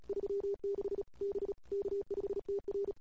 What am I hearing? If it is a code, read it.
Morse code, heard as a keyed tone, about 31 wpm, 310 Hz. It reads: 36 B K5TR
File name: CW / 2008-ss-cw-k5tr-op-n5rz -21:00